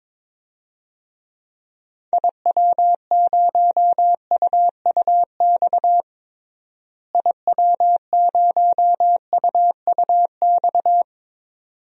Morse code, heard as a keyed tone, about 22 wpm, 700 Hz. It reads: IW0UUX IW0UUX
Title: DXCC-100 Call Signs - 6 Characters - Repeated 2x - 22wpm